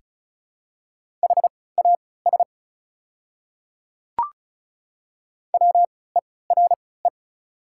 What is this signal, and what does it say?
Morse code, heard as a keyed tone, about 36 wpm, spaced as for 20 wpm, 690 Hz. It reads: HAS WERE
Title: 3x Character Spacing - Top 100 Words - 35wpm